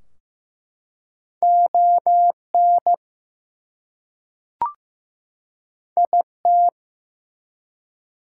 Morse code, heard as a keyed tone, about 15 wpm, 700 Hz. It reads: ON IT
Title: Two Letter Words - 15wpm